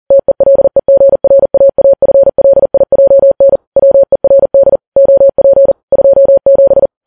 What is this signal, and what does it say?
Morse code, heard as a keyed tone, about 40 wpm, 550 Hz. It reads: TELEGRAAFLIJN WERD OP 27
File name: Files Web tls / Morse on9fv / morse40kps